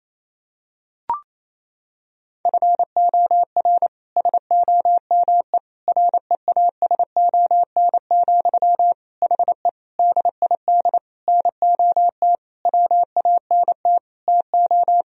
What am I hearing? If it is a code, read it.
Morse code, heard as a keyed tone, about 28 wpm, 700 Hz. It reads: FOR SOME REASON, HE DID NOT WANT TO
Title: Sentences from Top 1000 Words - Review Part 3 - 28wpm